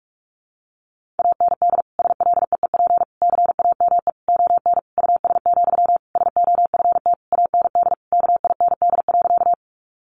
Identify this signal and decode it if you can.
Morse code, heard as a keyed tone, about 44 wpm, 700 Hz. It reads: AND SLEEP CAME ON US, SOFT AND KIND.
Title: Sentences from Top 1500 Words 45wpm